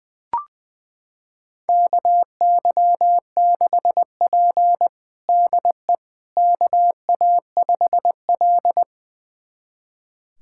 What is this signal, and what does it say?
Morse code, heard as a keyed tone, about 21 wpm, 700 Hz. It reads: KY6P DE KA5L
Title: USA Callsign DE Callsign - 4 Characters - 20wpm